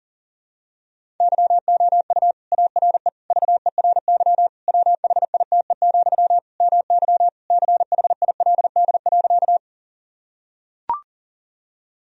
Morse code, heard as a keyed tone, about 40 wpm, 700 Hz. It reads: YOU ARE VERY WHITE, MY CHILD.
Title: Sentences from Top 1000 Words - Review Part 1 - 40wpm